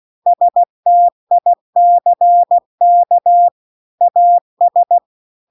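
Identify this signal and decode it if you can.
Morse code, heard as a keyed tone, about 16 wpm, 700 Hz. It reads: STICK AS
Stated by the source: Morse practice text at 16 WPM.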